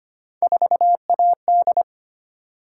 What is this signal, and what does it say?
Morse code, heard as a keyed tone, about 25 wpm, 700 Hz. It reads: 4AD
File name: Three Letter-Number Combinations - 25wpm